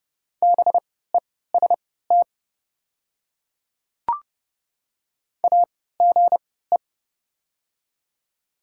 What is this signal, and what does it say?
Morse code, heard as a keyed tone, about 30 wpm, spaced as for 17 wpm, 690 Hz. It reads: BEST AGE
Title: 3x Character Spacing - Top 500 Words - 30wpm